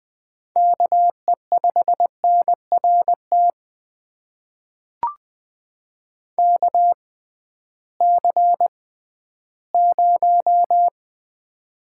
Morse code, heard as a keyed tone, about 20 wpm, spaced as for 13 wpm, 700 Hz. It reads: KE5NRT KC0
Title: USA Call Signs - 6 Characters - 6x Character Spacing - 20wpm